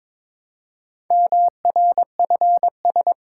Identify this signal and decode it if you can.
Morse code, heard as a keyed tone, about 22 wpm, 700 Hz. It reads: MRFS
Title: Code Groups - 5 Letters - 22wpm